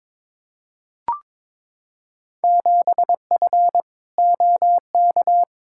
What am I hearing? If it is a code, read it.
Morse code, heard as a keyed tone, about 22 wpm, 700 Hz. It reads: 7F OK
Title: Field Day Exchanges Encore - 22wpm